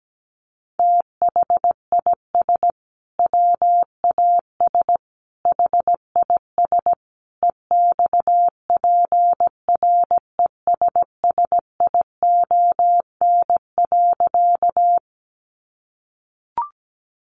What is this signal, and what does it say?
Morse code, heard as a keyed tone, about 17 wpm, 700 Hz. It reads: THIS WAS HIS EXPRESSION.
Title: Sentences from Top 1200 Words 17wpm